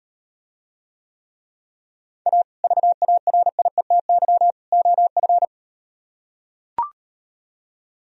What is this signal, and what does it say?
Morse code, heard as a keyed tone, about 38 wpm, 700 Hz. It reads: A VARIETY OF